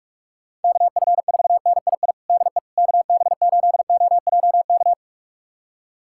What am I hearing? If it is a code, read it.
Morse code, heard as a keyed tone, about 46 wpm, 700 Hz. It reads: KF4NII DE KB8OJK
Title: USA Callsign DE Callsign - 6 Characters - 45wpm